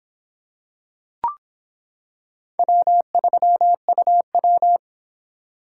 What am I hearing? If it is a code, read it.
Morse code, heard as a keyed tone, about 26 wpm, 700 Hz. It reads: W3UW